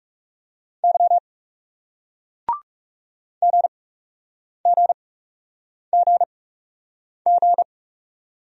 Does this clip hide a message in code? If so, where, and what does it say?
Yes: Morse code from 0.8 to 7.6 s, heard as a keyed tone, about 40 wpm, spaced as for 8 wpm, 700 Hz. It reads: Y GGGG